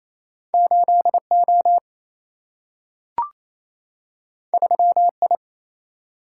Morse code, heard as a keyed tone, about 28 wpm, 700 Hz. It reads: 8O 3I
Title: Single-Letter Single-Number 28wpm